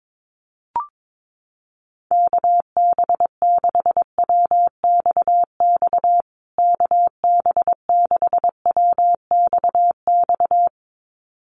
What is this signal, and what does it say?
Morse code, heard as a keyed tone, about 22 wpm, 700 Hz. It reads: KB6WXX KB6WXX
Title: USA Call Signs Repeated 2x 22wpm